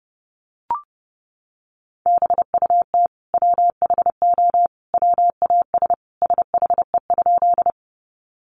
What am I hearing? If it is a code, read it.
Morse code, heard as a keyed tone, about 30 wpm, 700 Hz. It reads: BUT WHO WAS SHE?
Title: Sentences from Top 100 Words 30wpm